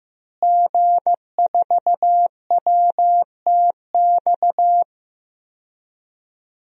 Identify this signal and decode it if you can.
Morse code, heard as a keyed tone, about 15 wpm, 700 Hz. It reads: G4WTX
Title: DXCC-100 Call Signs - 5 Characters - Repeated 3x - 15wpm